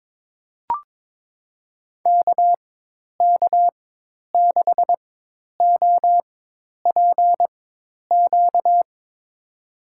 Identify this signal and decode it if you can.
Morse code, heard as a keyed tone, about 22 wpm, spaced as for 10 wpm, 700 Hz. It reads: KK6OPQ
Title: USA Call Signs - 6 Characters - 4x Character Spacing - 22wpm